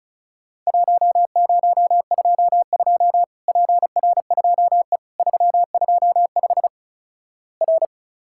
Morse code, heard as a keyed tone, about 36 wpm, 700 Hz. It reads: 1022 PR2E 325 R
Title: Worked All Europe DX Contest - QTC Exchange - 35wpm